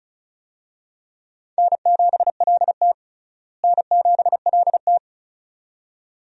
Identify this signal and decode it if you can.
Morse code, heard as a keyed tone, about 36 wpm, 700 Hz. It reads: N7LT N7LT
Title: USA Call Signs - 4 Characters - Repeated 2x - 35wpm